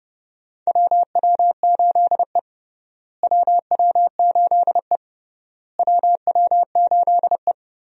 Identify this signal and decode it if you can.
Morse code, heard as a keyed tone, about 30 wpm, 700 Hz. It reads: WW8E WW8E WW8E
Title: USA Call Signs - 4 Characters - Repeated 3x - 30wpm